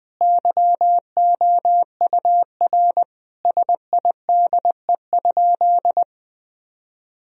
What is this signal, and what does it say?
Morse code, heard as a keyed tone, about 20 wpm, 700 Hz. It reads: YOUR SIDE?
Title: Sentences from Top 1000 Words - Review Part 3 - 20wpm